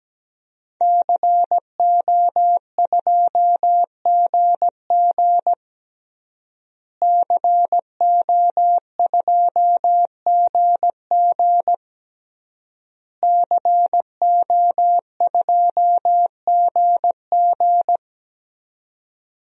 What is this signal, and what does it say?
Morse code, heard as a keyed tone, about 17 wpm, 700 Hz. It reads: CO2GG CO2GG CO2GG
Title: DXCC-100 Call Signs - 5 Characters - Repeated 3x - 17wpm